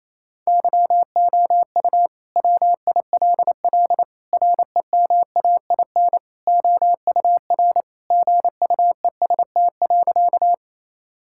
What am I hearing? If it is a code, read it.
Morse code, heard as a keyed tone, about 28 wpm, 700 Hz. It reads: YOU WILL REMAIN OUR GUEST.
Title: Sentences from Top 1600 Words 28wpm